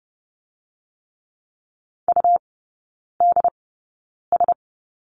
Morse code, heard as a keyed tone, about 30 wpm, spaced as for 9 wpm, 700 Hz. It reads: UDS
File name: Lesson 18e - Letter m - Three Characters 30wpm